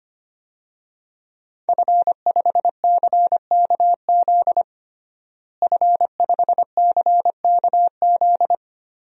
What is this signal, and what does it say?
Morse code, heard as a keyed tone, about 25 wpm, 700 Hz. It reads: F5CKZ F5CKZ
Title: DXCC-100 Call Signs - 5 Characters - Repeated 2x - 25wpm